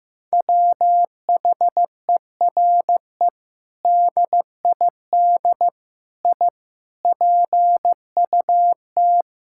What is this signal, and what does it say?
Morse code, heard as a keyed tone, about 16 wpm, 700 Hz. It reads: WHERE DID I PUT
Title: Sentences from Top 1600 Words 15wpm